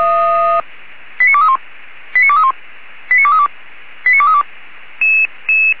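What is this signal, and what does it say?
Five-tone selective call (ZVEI), heard as quick runs of five tones, tones of about 70 ms. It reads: 89231 89231 89232 89232
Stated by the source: ZVEI five-tone selective call received by SDR, code 89243.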